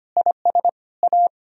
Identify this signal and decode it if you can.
Morse code, heard as a keyed tone, about 32 wpm, 700 Hz. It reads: IS A